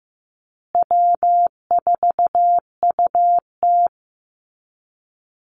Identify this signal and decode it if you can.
Morse code, heard as a keyed tone, about 15 wpm, 700 Hz. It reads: W4UT
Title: Easy Call Signs Repeated 3x - 15wpm